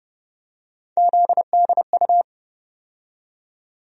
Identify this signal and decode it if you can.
Morse code, heard as a keyed tone, about 32 wpm, 700 Hz. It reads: ZDU
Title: USA Call Sign Postfixes - 3 Letters 30wpm